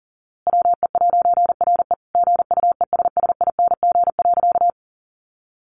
Morse code, heard as a keyed tone, about 37 wpm, 700 Hz. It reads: WE'RE GUESSING.
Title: A Sound Of Thunder 40wpm